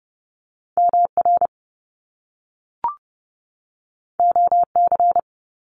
Morse code, heard as a keyed tone, about 30 wpm, 700 Hz. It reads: MR OC
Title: Top Two Letter Combinations in Words 30wpm